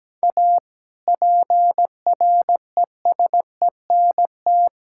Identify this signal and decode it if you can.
Morse code, heard as a keyed tone, about 17 wpm, 700 Hz. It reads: A PRESENT
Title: Sentences from Top 1000 Words - Review Part 5 - 17wpm